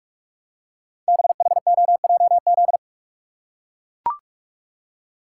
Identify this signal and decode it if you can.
Morse code, heard as a keyed tone, about 44 wpm, 690 Hz. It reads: BHOJZ
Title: Code Groups - 5 Letters - 45wpm